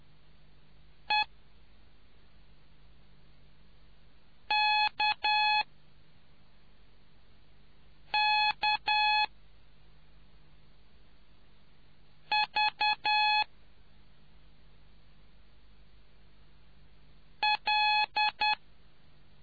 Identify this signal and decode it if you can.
Morse code, heard as a keyed tone, about 10 wpm, 850 Hz. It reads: E K K V L